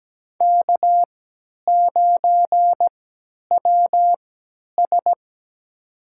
Morse code, heard as a keyed tone, about 17 wpm, spaced as for 10 wpm, 700 Hz. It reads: K9WS
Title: USA Call Signs - 4 Characters - 3x Character Spacing - 17wpm